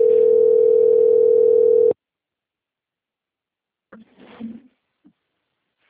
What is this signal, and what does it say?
Ringback tone (North America), 440 + 480 Hz.
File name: out- - - - - .31384